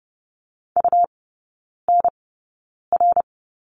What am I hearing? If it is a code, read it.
Morse code, heard as a keyed tone, about 30 wpm, spaced as for 9 wpm, 700 Hz. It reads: UNR
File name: Lesson 17e - Letter u - Three Characters 30wpm